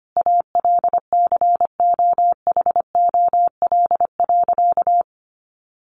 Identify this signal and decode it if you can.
Morse code, heard as a keyed tone, about 25 wpm, 700 Hz. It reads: ALCOHOL.
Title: Sentences from Top 1400 Words 25wpm